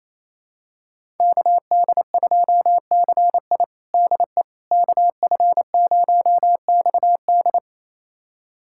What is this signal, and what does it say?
Morse code, heard as a keyed tone, about 28 wpm, 700 Hz. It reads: KD2CI DE KF0XD
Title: USA Callsign DE Callsign - 5 Characters - 28wpm